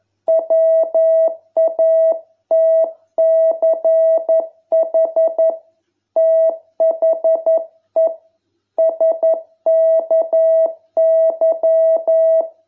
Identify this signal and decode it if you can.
Morse code, heard as a keyed tone, about 11 wpm, 660 Hz. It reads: WATCH THE SKY